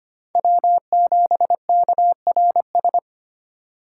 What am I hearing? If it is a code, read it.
Morse code, heard as a keyed tone, about 25 wpm, 700 Hz. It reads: W7KRS
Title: K1USN SST QSO Practice - 25wpm